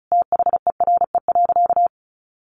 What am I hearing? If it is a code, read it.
Morse code, heard as a keyed tone, about 34 wpm, 700 Hz. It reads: THERE.
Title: Sentences from Top 800 Words 35wpm